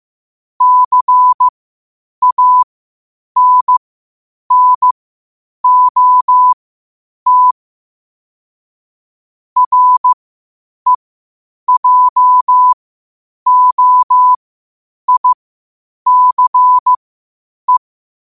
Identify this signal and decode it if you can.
Morse code, heard as a keyed tone, about 15 wpm, spaced as for 8 wpm, 1000 Hz. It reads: CANNOT REJOICE
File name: 8wpm